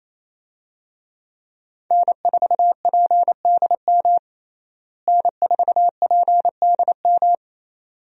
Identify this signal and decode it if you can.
Morse code, heard as a keyed tone, about 27 wpm, 700 Hz. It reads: N4PDM N4PDM
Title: USA Call Signs - 5 Characters - Repeated 2x - 28wpm